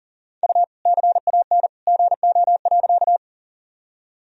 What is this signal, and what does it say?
Morse code, heard as a keyed tone, about 40 wpm, 700 Hz. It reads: U CAN GO.
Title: Sentences from Top 100 Words and Abbreviations 40wpm